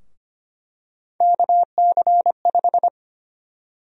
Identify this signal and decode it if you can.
Morse code, heard as a keyed tone, about 25 wpm, 700 Hz. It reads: KC5
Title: USA Call Sign Prefixes - Repeated 2x - 25wpm